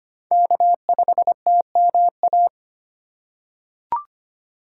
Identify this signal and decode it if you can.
Morse code, heard as a keyed tone, about 25 wpm, 700 Hz. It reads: K5TMA
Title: USA Call Signs - 5 Characters - 4x Character Spacing - 25wpm